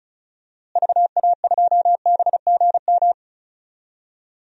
Morse code, heard as a keyed tone, about 36 wpm, 700 Hz. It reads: VA2BGM
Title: DXCC-100 Call Signs - 6 Characters - Repeated 2x - 35wpm